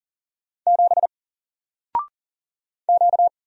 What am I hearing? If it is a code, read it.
Morse code, heard as a keyed tone, about 40 wpm, 700 Hz. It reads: 7 Q
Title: Single Letter-Number 40wpm